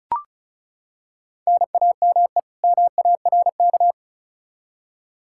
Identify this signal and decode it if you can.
Morse code, heard as a keyed tone, about 36 wpm, 700 Hz. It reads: NAME MARK
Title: Practice QSO - Exchange 1 - Beginning - 35wpm